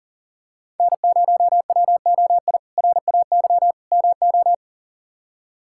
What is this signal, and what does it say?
Morse code, heard as a keyed tone, about 40 wpm, 700 Hz. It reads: N0WOI RAY MO
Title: K1USN SST QSO Practice - 40wpm